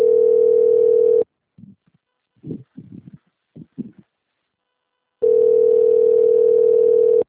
Ringback tone (North America), 440 + 480 Hz.